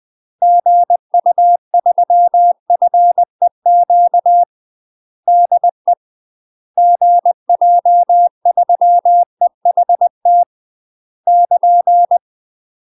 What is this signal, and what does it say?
Morse code, heard as a keyed tone, about 20 wpm, 700 Hz. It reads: GU3FEQ DE GJ3EHT <KN>
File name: UK MASTER QSOs TRAINING / 20WPM / UK MASTER QSOs 0002